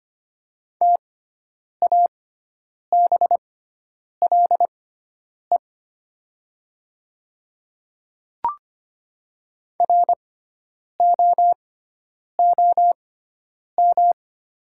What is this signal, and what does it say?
Morse code, heard as a keyed tone, about 25 wpm, spaced as for 9 wpm, 700 Hz. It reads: TABLE ROOM
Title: 6x Character Spacing - Top 500 Words - 25wpm